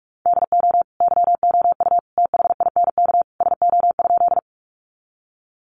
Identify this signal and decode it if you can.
Morse code, heard as a keyed tone, about 44 wpm, 700 Hz. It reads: DO YOU THINK SO?
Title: Sentences from Top 200 Words 45wpm